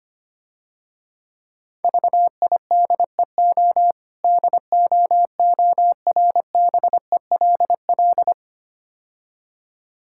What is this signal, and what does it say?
Morse code, heard as a keyed tone, about 25 wpm, 700 Hz. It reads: VIDEO DOORBELL
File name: practice / 25wpm / Difficult Words - Speed-Racing with Hints - 25wpm